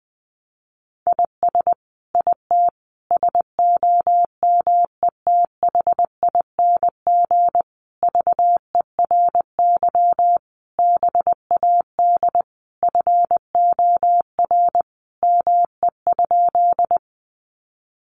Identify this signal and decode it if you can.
Morse code, heard as a keyed tone, about 20 wpm, 700 Hz. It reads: IS IT SOMETHING VERY BAD FOR ME?
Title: Sentences from Top 700 Words 20wpm